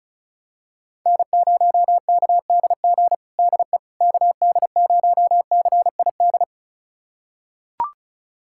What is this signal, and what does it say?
Morse code, heard as a keyed tone, about 34 wpm, 700 Hz. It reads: N0KDG DE KD0CID
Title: USA Callsign DE Callsign 35wpm